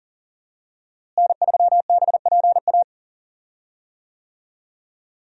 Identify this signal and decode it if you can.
Morse code, heard as a keyed tone, about 39 wpm, 700 Hz. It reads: N3BPA
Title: USA Call Signs - 5 Characters - Repeated 2x - 40wpm